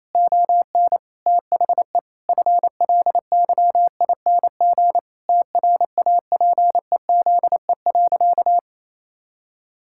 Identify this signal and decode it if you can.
Morse code, heard as a keyed tone, about 28 wpm, 700 Hz. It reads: ON THE FLYING TRAPEZE.
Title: Pangram Challenge v2 - 28wpm